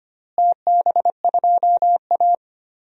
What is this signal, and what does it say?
Morse code, heard as a keyed tone, about 25 wpm, 700 Hz. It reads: TB2A
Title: DXCC-100 Call Signs - 4 Characters - Repeated 3x - 25wpm